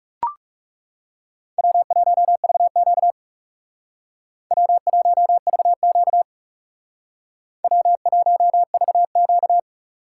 Morse code, heard as a keyed tone, about 34 wpm, 700 Hz. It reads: W1VQ W1VQ W1VQ
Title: Speed Racing - USA 1x2 Call Signs 30wpm